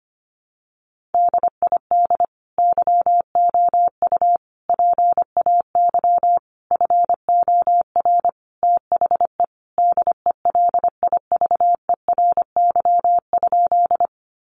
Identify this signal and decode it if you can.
Morse code, heard as a keyed tone, about 25 wpm, 700 Hz. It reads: DID YOU PAY FOR THE DELIVERY?
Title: Sentences from Top 1500 Words 25wpm